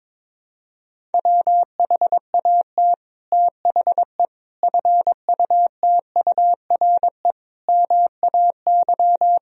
Morse code, heard as a keyed tone, about 21 wpm, 700 Hz. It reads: WHAT THE FUTURE MAY
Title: Sentences from Top 1000 Words - Review Part 2 - 22wpm